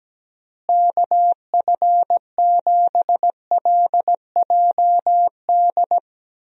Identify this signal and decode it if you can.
Morse code, heard as a keyed tone, about 17 wpm, 700 Hz. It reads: KF7LJD
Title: USA Call Signs - 6 Characters - Repeated 2x - 17wpm